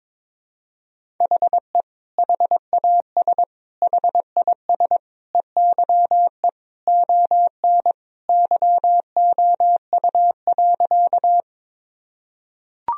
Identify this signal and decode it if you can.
Morse code, heard as a keyed tone, about 22 wpm, 700 Hz. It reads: HE HAS HIS EYE ON YOU.
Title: Sentences from Top 1000 Words - Review Part 1 - 22wpm